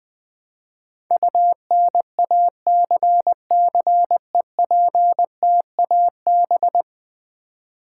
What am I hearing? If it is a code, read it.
Morse code, heard as a keyed tone, about 20 wpm, 700 Hz. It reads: UNACCEPTAB
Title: Build A Word - 5 Letters At A Time - 12-Letter Words - 20wpm